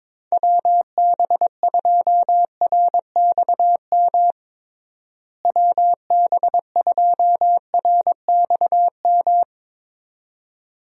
Morse code, heard as a keyed tone, about 22 wpm, 700 Hz. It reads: WB2RXM WB2RXM
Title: USA Call Signs - 6 Characters - Repeated 2x - 22wpm